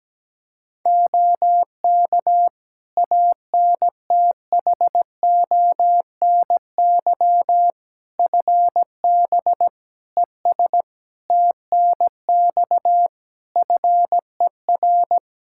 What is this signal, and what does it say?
Morse code, heard as a keyed tone, about 17 wpm, 700 Hz. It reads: OK ANTHONY FB ES TNX FER
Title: Practice QSO - Exchange 2 - Info - 17wpm